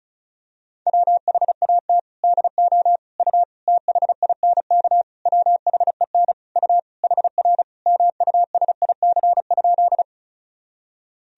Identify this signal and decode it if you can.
Morse code, heard as a keyed tone, about 36 wpm, 700 Hz. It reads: WHAT DO U THINK WHEN U HR MUSIC?
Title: Sentences from Top 300 Words and Abbreviations 35wpm